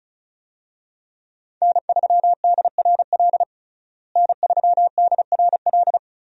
Morse code, heard as a keyed tone, about 34 wpm, 700 Hz. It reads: N3DRL N3DRL
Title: USA Call Signs - 5 Characters - Repeated 2x - 35wpm